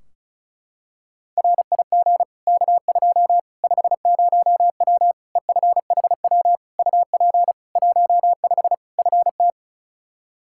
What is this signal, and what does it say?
Morse code, heard as a keyed tone, about 36 wpm, 700 Hz. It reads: RIG K2 50W EFHW UP 15 FT
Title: Practice QSO - Exchange 2 - Info - 35wpm